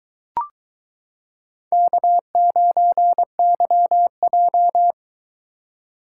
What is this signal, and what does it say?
Morse code, heard as a keyed tone, about 22 wpm, 700 Hz. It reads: K9YJ